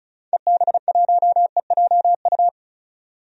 Morse code, heard as a keyed tone, about 34 wpm, 700 Hz. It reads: EB1EJU
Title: DXCC-100 Call Signs - 6 Characters - 35wpm